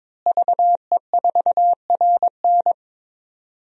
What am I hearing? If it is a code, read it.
Morse code, heard as a keyed tone, about 22 wpm, 700 Hz. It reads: VE4RN
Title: DXCC-100 Call Signs - 5 Characters - Repeated 3x - 22wpm